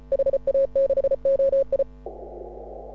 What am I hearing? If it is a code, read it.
Morse code, heard as a keyed tone, about 33 wpm, 530 Hz. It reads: HA6OI